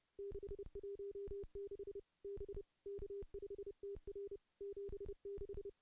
Morse code, heard as a keyed tone, about 30 wpm, 400 Hz. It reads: 616 B K5TR 76
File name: contests / CW-SS-k5tr- -0400